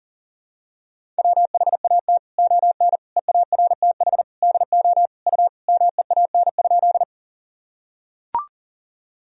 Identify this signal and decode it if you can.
Morse code, heard as a keyed tone, about 40 wpm, 700 Hz. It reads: WHAT ON EARTH DO U MEAN?
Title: Sentences from Top 200 Words and Abbreviations 40wpm